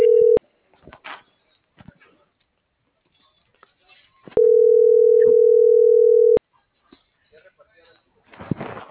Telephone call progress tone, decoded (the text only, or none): ringback tone